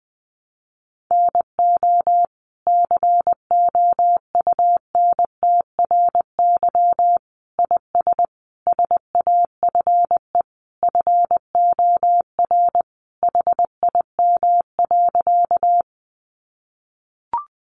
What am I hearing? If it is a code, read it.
Morse code, heard as a keyed tone, about 20 wpm, 700 Hz. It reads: NO COUNTRY IS SAFE FOR HIM.
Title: Sentences from Top 900 Words 20wpm